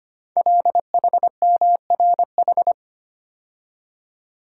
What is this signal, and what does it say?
Morse code, heard as a keyed tone, about 25 wpm, 700 Hz. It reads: LHMRH
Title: Code Groups - 5 Letters - 25wpm